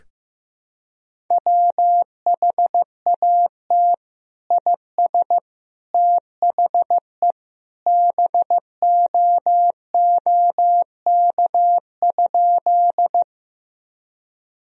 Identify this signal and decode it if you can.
Morse code, heard as a keyed tone, about 15 wpm, 700 Hz. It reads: WHAT IS THE BOOK?
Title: Sentences from Top 1000 Words - Review Part 1 - 15wpm